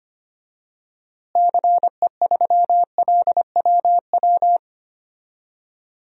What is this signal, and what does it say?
Morse code, heard as a keyed tone, about 25 wpm, 700 Hz. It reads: CE3LWW
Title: DXCC-100 Call Signs - 6 Characters - Repeated 3x - 25wpm